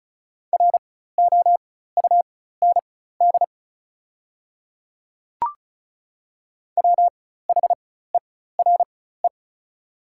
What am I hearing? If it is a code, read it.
Morse code, heard as a keyed tone, about 36 wpm, spaced as for 16 wpm, 700 Hz. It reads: ROUND WHERE
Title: 4x Character Spacing - Top 200 Words - 35wpm